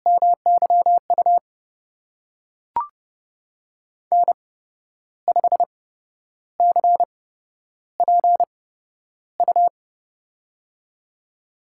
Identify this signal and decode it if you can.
Morse code, heard as a keyed tone, about 30 wpm, spaced as for 11 wpm, 700 Hz. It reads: MYU N5CPU
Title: USA Call Signs - 5 Characters - 8x Character Spacing - 30wpm